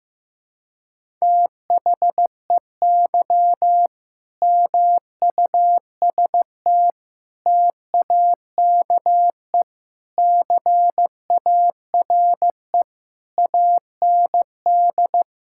Morse code, heard as a keyed tone, about 16 wpm, 700 Hz. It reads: THEY MUST TAKE CARE AND
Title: Sentences from Top 1000 Words - Review Part 6 - 15wpm